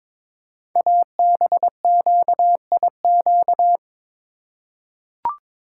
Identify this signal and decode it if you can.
Morse code, heard as a keyed tone, about 23 wpm, 700 Hz. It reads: ABQIQ
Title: Code Groups - 5 Letters - 22wpm